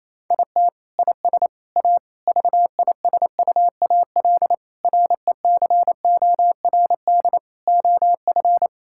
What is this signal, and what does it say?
Morse code, heard as a keyed tone, about 28 wpm, 700 Hz. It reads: IT IS A VISUAL RECORD OF
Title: Sentences from Top 1000 Words - Review Part 6 - 28wpm